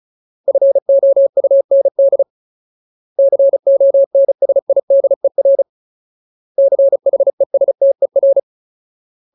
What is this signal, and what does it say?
Morse code, heard as a keyed tone, about 34 wpm, 550 Hz. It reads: FOUND CONSIDER CHESTER
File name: Ultimate Word List / 35WPM Ultimate